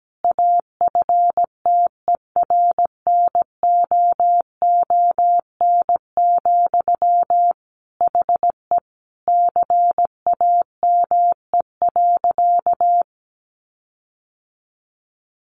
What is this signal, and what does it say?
Morse code, heard as a keyed tone, about 17 wpm, 700 Hz. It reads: AFTERNOON, HE CAME.
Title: Sentences from Top 1500 Words 17wpm